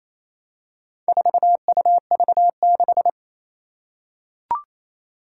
Morse code, heard as a keyed tone, about 28 wpm, 700 Hz. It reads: 4UV6
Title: Four Challenging Letter-Number Combinations 28wpm